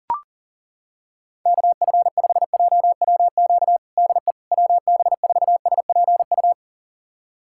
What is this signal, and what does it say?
Morse code, heard as a keyed tone, about 37 wpm, 700 Hz. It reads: KF5JWQ DE WB4SPU
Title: USA Callsign DE Callsign - 6 Characters - 40wpm